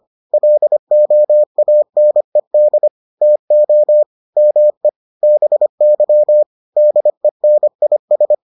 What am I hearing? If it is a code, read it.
Morse code, heard as a keyed tone, about 25 wpm, 600 Hz. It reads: LOANED TO ME BY DENIS